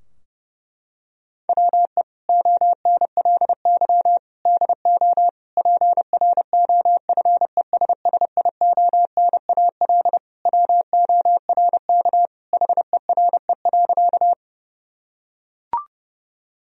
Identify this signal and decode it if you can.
Morse code, heard as a keyed tone, about 28 wpm, 700 Hz. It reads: WE ONLY DO PROFESSIONAL WORK HERE.
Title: Sentences from Top 1000 Words - Review Part 4 - 30wpm